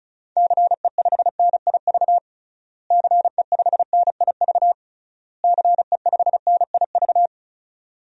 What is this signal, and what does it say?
Morse code, heard as a keyed tone, about 34 wpm, 700 Hz. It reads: CE5NIV CE5NIV CE5NIV
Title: DXCC-100 Call Signs - 6 Characters - Repeated 3x - 35wpm